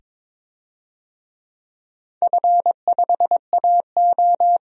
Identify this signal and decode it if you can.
Morse code, heard as a keyed tone, about 22 wpm, 700 Hz. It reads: F5AO
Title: DXCC-100 Call Signs - 4 Characters - 22wpm